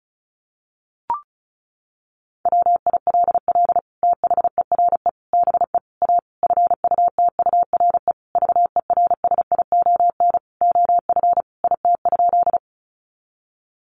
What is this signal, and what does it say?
Morse code, heard as a keyed tone, about 34 wpm, 700 Hz. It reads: WILL THERE BE A FUTURE VERSION OF IT?
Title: Sentences from Top 1400 Words 35wpm